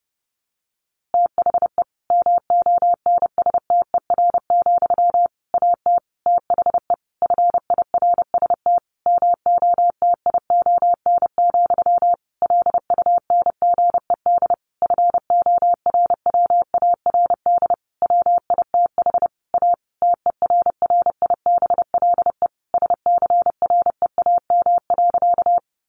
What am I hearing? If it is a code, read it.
Morse code, heard as a keyed tone, about 30 wpm, 700 Hz. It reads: THE MONSTER, AT THE FIRST MOTION, LUNGED FORWARD WITH A TERRIBLE SCREAM.
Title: A Sound Of Thunder 30wpm